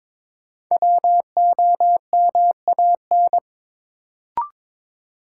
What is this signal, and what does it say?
Morse code, heard as a keyed tone, about 22 wpm, 700 Hz. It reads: WOMAN
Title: Sentences from Top 1100 Words - New Word Drill Alt - 22wpm